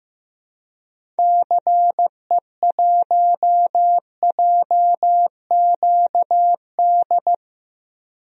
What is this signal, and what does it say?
Morse code, heard as a keyed tone, about 15 wpm, 700 Hz. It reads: CE1JQD
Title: DXCC-100 Call Signs - 6 Characters - 15wpm